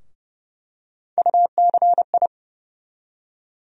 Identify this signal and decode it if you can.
Morse code, heard as a keyed tone, about 30 wpm, 700 Hz. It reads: UCI
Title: USA Call Sign Postfixes - 3 Letters 30wpm